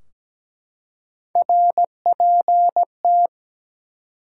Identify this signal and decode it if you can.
Morse code, heard as a keyed tone, about 17 wpm, 700 Hz. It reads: RPT
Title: Top 180 QSO Elements 17wpm